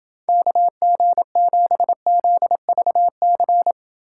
Morse code, heard as a keyed tone, about 29 wpm, 700 Hz. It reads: KG7ZVC